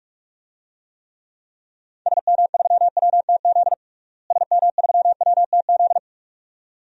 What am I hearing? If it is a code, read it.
Morse code, heard as a keyed tone, about 41 wpm, 700 Hz. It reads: SM3WTZ SM3WTZ
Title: DXCC-100 Call Signs - 6 Characters - Repeated 2x - 45wpm